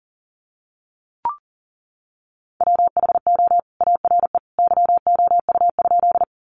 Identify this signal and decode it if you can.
Morse code, heard as a keyed tone, about 40 wpm, 700 Hz. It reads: WHO ARE YOU?
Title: Sentences from Top 100 Words 40wpm